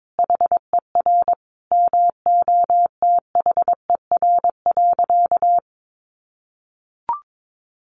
Morse code, heard as a keyed tone, about 22 wpm, 700 Hz. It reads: HER MOTHER.
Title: Sentences from Top 400 Words 22wpm